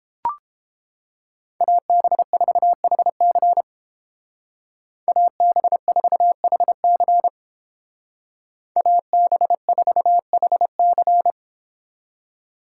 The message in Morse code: AB4HC AB4HC AB4HC

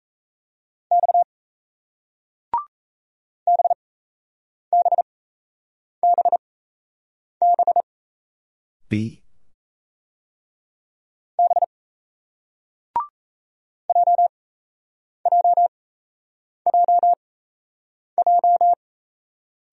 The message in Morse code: X BBBB B JJJJ

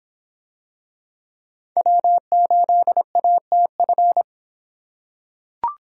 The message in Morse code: W8ATF

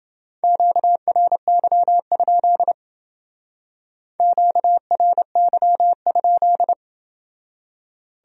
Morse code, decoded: QRY? QRY?

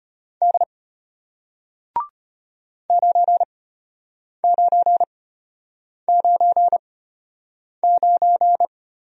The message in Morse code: D 9999

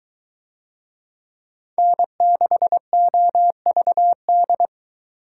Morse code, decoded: N6OVD

